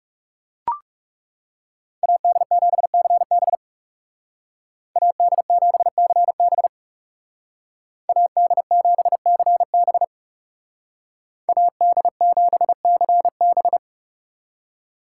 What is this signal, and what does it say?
Morse code, heard as a keyed tone, about 40 wpm, 700 Hz. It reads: AD7CB AD7CB AD7CB AD7CB